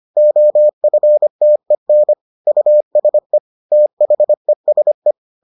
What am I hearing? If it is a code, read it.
Morse code, heard as a keyed tone, about 25 wpm, 600 Hz. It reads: OFTEN USE THESE